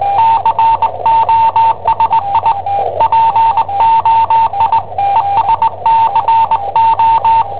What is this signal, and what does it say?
Morse code, heard as a keyed tone, about 20 wpm, 900 Hz, with radio noise. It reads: COSI POI ESCO
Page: Cassetta esercizi preparata da I7OHP per IW5ELP